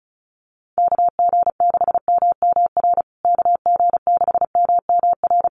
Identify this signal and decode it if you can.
Morse code, heard as a keyed tone, about 34 wpm, 700 Hz. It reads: KG6MMR KG6MMR
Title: USA Call Signs Repeated 3x 35wpm